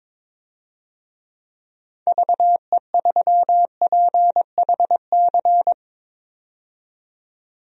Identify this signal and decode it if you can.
Morse code, heard as a keyed tone, about 22 wpm, 700 Hz. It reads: VE3PHC
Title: DXCC-100 Call Signs - 6 Characters - 22wpm